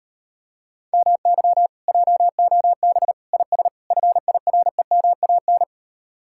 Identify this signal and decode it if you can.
Morse code, heard as a keyed tone, about 31 wpm, 700 Hz. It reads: MY JOB IS FIREMAN
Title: Speed-Racing Common QSO Phrases - 25wpm